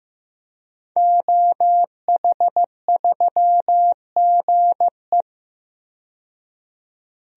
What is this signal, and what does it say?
Morse code, heard as a keyed tone, about 16 wpm, 700 Hz. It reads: OH3GE
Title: DXCC-100 Call Signs - 5 Characters - 15wpm